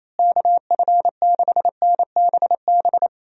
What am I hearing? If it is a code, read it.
Morse code, heard as a keyed tone, about 28 wpm, 700 Hz. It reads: KF6NBB
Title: USA Call Signs - 6 Characters - Repeated 3x - 28wpm